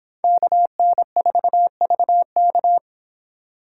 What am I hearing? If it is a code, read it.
Morse code, heard as a keyed tone, about 26 wpm, 700 Hz. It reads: KN4VK